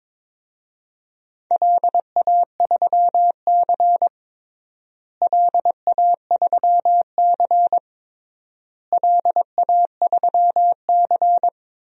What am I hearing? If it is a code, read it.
Morse code, heard as a keyed tone, about 22 wpm, 700 Hz. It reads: LA3C LA3C LA3C
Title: DXCC-100 Call Signs - 4 Characters - Repeated 3x - 22wpm